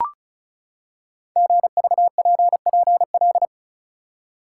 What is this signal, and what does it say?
Morse code, heard as a keyed tone, about 36 wpm, 700 Hz. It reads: GVPPL
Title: Code Groups - 5 Letters - 35wpm